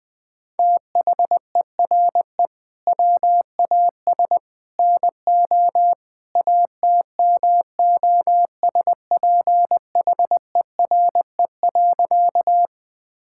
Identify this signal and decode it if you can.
Morse code, heard as a keyed tone, about 20 wpm, 700 Hz. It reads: THERE WAS NO ATMOSPHERE.
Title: Sentences from Top 1600 Words 20wpm